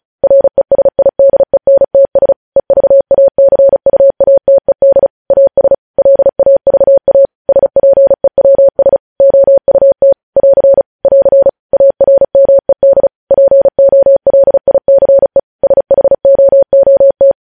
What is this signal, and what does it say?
Morse code, heard as a keyed tone, about 36 wpm, 550 Hz. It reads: RESIDENTS EVACUATED AS LAVA SPEWS OUT <AR> <AR> ARMED POLICE SHOOT
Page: news headlines in morse code at 35 wpm